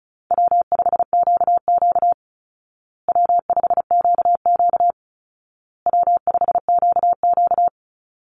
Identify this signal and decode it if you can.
Morse code, heard as a keyed tone, about 34 wpm, 700 Hz. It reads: W5QQ W5QQ W5QQ
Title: Call Signs Repeated 2x - 35wpm